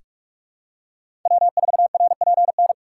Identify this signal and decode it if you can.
Morse code, heard as a keyed tone, about 46 wpm, 700 Hz. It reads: W4RPN
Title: USA Call Signs - 5 Characters - 2x Character Spacing - 45wpm